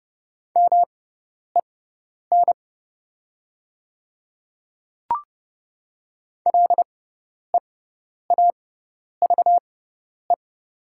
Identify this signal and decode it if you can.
Morse code, heard as a keyed tone, about 30 wpm, spaced as for 10 wpm, 700 Hz. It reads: MEN LEAVE
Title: 6x Character Spacing - Top 500 Words - 30wpm